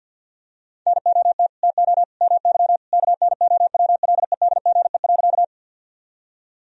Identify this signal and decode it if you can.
Morse code, heard as a keyed tone, about 51 wpm, 700 Hz. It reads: NOT TO MY KNOWLEDGE.
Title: Sentences from Top 1000 Words - Review Part 3 - 50wpm